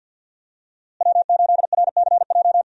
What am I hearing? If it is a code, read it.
Morse code, heard as a keyed tone, about 53 wpm, 700 Hz. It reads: W8RCJ
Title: USA Call Signs - 5 Characters - Repeated 3x - 50wpm